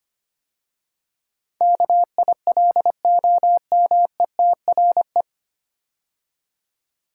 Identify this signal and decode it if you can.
Morse code, heard as a keyed tone, about 25 wpm, 700 Hz. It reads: KILOMETRE
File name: practice / 25wpm / 9 Letter Words - First 3 Letters Given - 25wpm